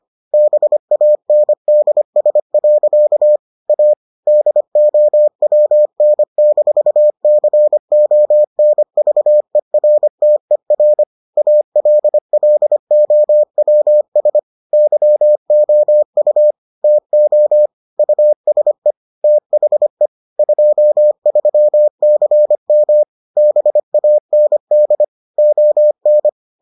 Morse code, heard as a keyed tone, about 25 wpm, 600 Hz. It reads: BANDS. A DOWN-CONVERTER ALLOWS YOU TO USE THE 23CM BAND ON